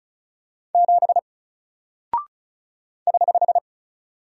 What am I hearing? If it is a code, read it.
Morse code, heard as a keyed tone, about 34 wpm, 700 Hz. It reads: 7 <HH>
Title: Single Character (Letter:Number:Punctuation:Prosign) Alt 35wpm